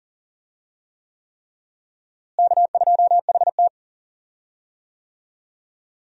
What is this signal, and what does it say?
Morse code, heard as a keyed tone, about 40 wpm, 700 Hz. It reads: K2HT
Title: USA Call Signs - 4 Characters - Repeated 3x - 40wpm